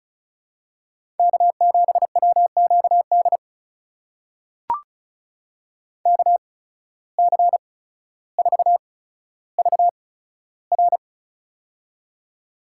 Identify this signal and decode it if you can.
Morse code, heard as a keyed tone, about 36 wpm, spaced as for 15 wpm, 700 Hz. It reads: K7WQD KC4VR
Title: USA Call Signs - 5 Characters - 8x Character Spacing - 35wpm